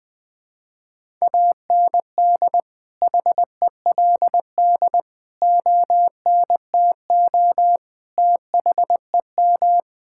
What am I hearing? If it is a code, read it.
Morse code, heard as a keyed tone, about 20 wpm, 700 Hz. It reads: AND HELD ONTO THEM
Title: Sentences with 9-Word Phrases - Spoken Beginning - 20wpm